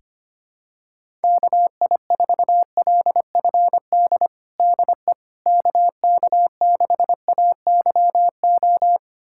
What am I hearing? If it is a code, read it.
Morse code, heard as a keyed tone, about 25 wpm, 700 Hz. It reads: KI4LFD DE KK6AYO
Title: USA Callsign DE Callsign 25wpm